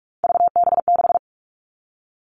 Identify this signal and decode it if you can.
Morse code, heard as a keyed tone, about 46 wpm, 700 Hz. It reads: VB6
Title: Three Challenging Letter-Number Combinations 45wpm